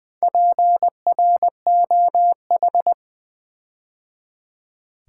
Morse code, heard as a keyed tone, about 20 wpm, 700 Hz. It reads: PROH
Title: Build A Word - 4 Letters At A Time - 8-Letter Words - 20wpm